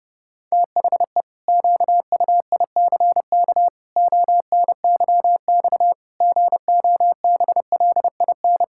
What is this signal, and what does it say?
Morse code, heard as a keyed tone, about 30 wpm, 700 Hz. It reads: THE QUICK ONYX GOBLIN
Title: Pangram Challenge v2 - 30wpm